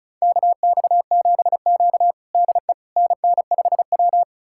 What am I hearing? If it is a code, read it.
Morse code, heard as a keyed tone, about 36 wpm, 700 Hz. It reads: KX7Q DE NN5W
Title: USA Callsign DE Callsign - 4 Characters - 35wpm